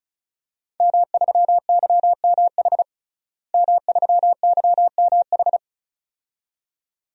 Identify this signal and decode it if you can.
Morse code, heard as a keyed tone, about 36 wpm, 700 Hz. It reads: M3YMH M3YMH
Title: DXCC-100 Call Signs - 5 Characters - Repeated 2x - 35wpm